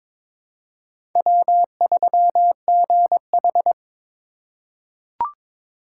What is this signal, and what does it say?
Morse code, heard as a keyed tone, about 21 wpm, 700 Hz. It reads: W3GH
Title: USA Call Signs - 2x Character Spacing - 22wpm